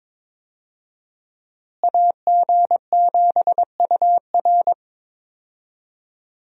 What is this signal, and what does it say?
Morse code, heard as a keyed tone, about 21 wpm, 700 Hz. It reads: AG7UR
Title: USA Call Signs - 5 Characters - Repeated 2x - 22wpm